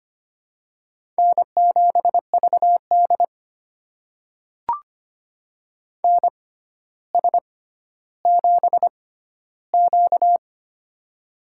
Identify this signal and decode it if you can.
Morse code, heard as a keyed tone, about 25 wpm, spaced as for 13 wpm, 700 Hz. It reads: N7VD NS7Q
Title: USA Call Signs - 4 Characters - 6x Character Spacing - 25wpm